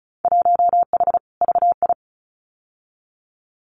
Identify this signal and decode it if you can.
Morse code, heard as a keyed tone, about 34 wpm, 700 Hz. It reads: 1H VI
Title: Winter Field Day Exchanges Encore 35wpm